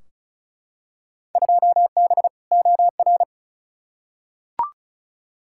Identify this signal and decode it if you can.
Morse code, heard as a keyed tone, about 34 wpm, 700 Hz. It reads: 2B OR
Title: Field Day Exchanges Encore - 35wpm